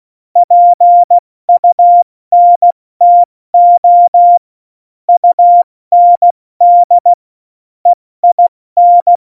Morse code, heard as a keyed tone, about 16 wpm, 700 Hz. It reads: PUNTO UND EIN
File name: Morsekurs / Witze / Witz 04 August 2020, 70 F80